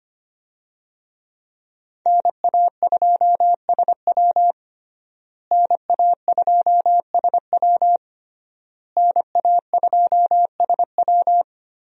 Morse code, heard as a keyed tone, about 25 wpm, 700 Hz. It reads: NA2SW NA2SW NA2SW
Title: USA Call Signs - 5 Characters - Repeated 3x - 25wpm